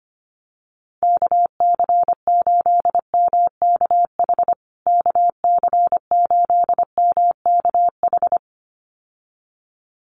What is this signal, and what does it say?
Morse code, heard as a keyed tone, about 25 wpm, 700 Hz. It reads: KC8MKH KC8MKH
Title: USA Call Signs Repeated 2x 25wpm